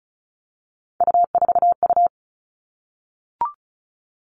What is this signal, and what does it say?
Morse code, heard as a keyed tone, about 36 wpm, 700 Hz. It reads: U4U
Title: Three Challenging Letter-Number Combinations 35wpm